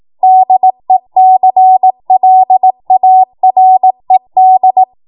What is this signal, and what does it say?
Morse code, heard as a keyed tone, about 18 wpm, 750 Hz. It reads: DECLARED
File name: Morse / 18WPM